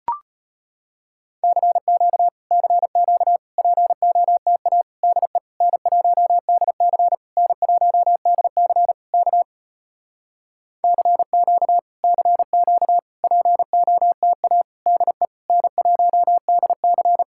Morse code, heard as a keyed tone, about 33 wpm, 700 Hz. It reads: CQ CQ POTA DE N1DC N1DC K CQ CQ POTA DE N1DC